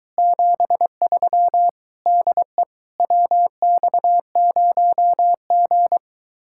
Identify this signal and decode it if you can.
Morse code, heard as a keyed tone, about 23 wpm, 700 Hz. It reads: 73 DE WX0G